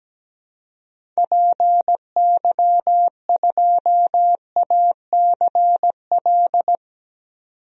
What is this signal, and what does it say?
Morse code, heard as a keyed tone, about 17 wpm, 700 Hz. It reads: PY2ACL
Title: DXCC-100 Call Signs - 6 Characters - 17wpm